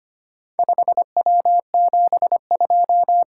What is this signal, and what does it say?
Morse code, heard as a keyed tone, about 25 wpm, 700 Hz. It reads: 5W72
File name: practice / 25wpm / Four Letter-Number Combinations - 25wpm